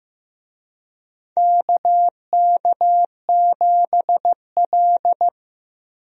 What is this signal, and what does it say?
Morse code, heard as a keyed tone, about 15 wpm, 700 Hz. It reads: KK7L
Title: USA Call Signs - 4 Characters - Repeated 3x - 15wpm